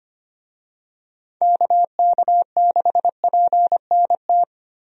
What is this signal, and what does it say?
Morse code, heard as a keyed tone, about 25 wpm, 700 Hz. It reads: KK6PNT
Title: USA Call Signs - 6 Characters - 25wpm